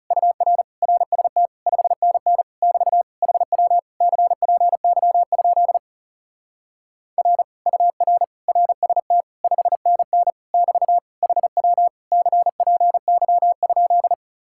Morse code, heard as a keyed tone, about 37 wpm, 700 Hz. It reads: UR RST 5NN <BT> HW CPY? R UR RST 5NN <BT> HW CPY?
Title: Speed-Racing Common QSO Phrases - 30wpm